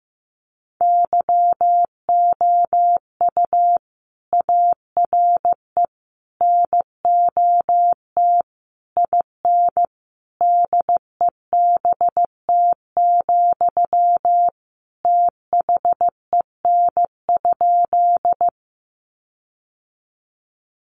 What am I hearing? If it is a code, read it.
Morse code, heard as a keyed tone, about 15 wpm, 700 Hz. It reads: YOU ARE NOT IN DEBT, THEN?
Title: Sentences from Top 1300 Words 15wpm